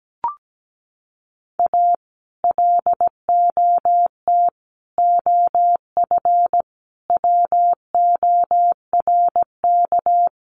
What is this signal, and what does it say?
Morse code, heard as a keyed tone, about 17 wpm, 700 Hz. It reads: A LOT OF WORK
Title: Sets of 5 Words Encore2 17wpm